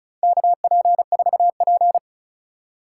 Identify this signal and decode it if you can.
Morse code, heard as a keyed tone, about 36 wpm, 700 Hz. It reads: KP4P
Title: USA Call Signs - 4 Characters - 4x Character Spacing - 35wpm